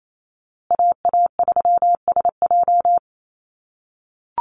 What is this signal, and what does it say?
Morse code, heard as a keyed tone, about 28 wpm, 700 Hz. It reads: AA3SJ
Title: USA Call Signs Repeated 3x 28wpm